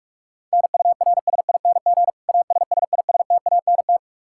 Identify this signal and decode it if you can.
Morse code, heard as a keyed tone, about 42 wpm, 700 Hz. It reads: NURSING ASSISTANT
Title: Practice QSO - Exchange 3 - More Info - 45wpm